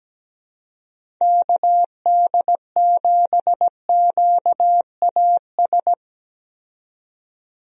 Morse code, heard as a keyed tone, about 17 wpm, 700 Hz. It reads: KD7QAS
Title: USA Call Signs - 6 Characters - 17wpm